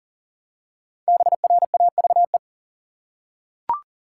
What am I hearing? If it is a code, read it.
Morse code, heard as a keyed tone, about 40 wpm, 700 Hz. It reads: BRAVE
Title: Five Letter Words - 40wpm